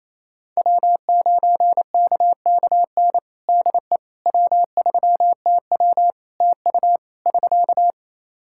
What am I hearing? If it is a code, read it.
Morse code, heard as a keyed tone, about 27 wpm, 700 Hz. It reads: W9KKN DE W3TW TU <SK>
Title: Practice QSO - Exchange 4 - Ending - 28wpm